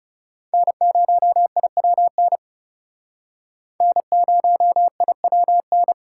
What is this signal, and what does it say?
Morse code, heard as a keyed tone, about 39 wpm, 700 Hz. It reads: N0IWN N0IWN